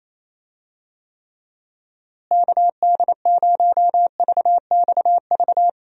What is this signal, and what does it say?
Morse code, heard as a keyed tone, about 28 wpm, 700 Hz. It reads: KD0VXV
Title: USA Call Signs - 6 Characters - Repeated 3x - 28wpm